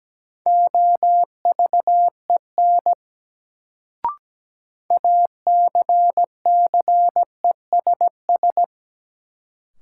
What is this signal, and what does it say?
Morse code, heard as a keyed tone, about 17 wpm, 700 Hz. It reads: OVEN ACCESS
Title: Sentences from Top 1000 Words - Review Part 4 - New Word Drill - 17wpm